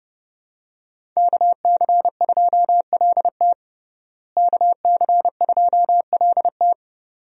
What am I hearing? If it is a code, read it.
Morse code, heard as a keyed tone, about 30 wpm, 700 Hz. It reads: KC2LT KC2LT
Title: USA Call Signs - 5 Characters - Repeated 2x - 30wpm